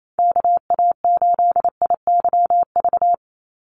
Morse code, heard as a keyed tone, about 27 wpm, 700 Hz. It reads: KA8IYV
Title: USA Call Signs 28wpm